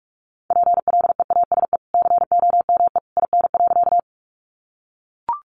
Morse code, heard as a keyed tone, about 44 wpm, 700 Hz. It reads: PLEASE COME IN.
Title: Sentences from Top 1500 Words 45wpm